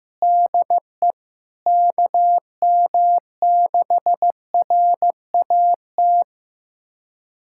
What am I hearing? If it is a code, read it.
Morse code, heard as a keyed tone, about 15 wpm, 700 Hz. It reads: DE KM6RAT
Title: USA Callsign DE Callsign - 6 Characters - 15wpm